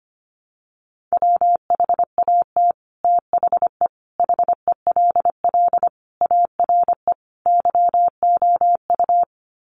Morse code, heard as a keyed tone, about 25 wpm, 700 Hz. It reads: WHAT THE HELL ARE YOU
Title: Sets of 5 Words Encore 25wpm